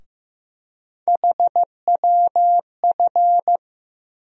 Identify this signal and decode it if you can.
Morse code, heard as a keyed tone, about 15 wpm, 700 Hz. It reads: HWF
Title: USA Call Sign Postfixes - 3 Letters 15wpm